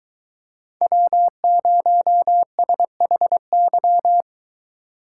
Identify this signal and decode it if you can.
Morse code, heard as a keyed tone, about 23 wpm, 700 Hz. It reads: W0SHY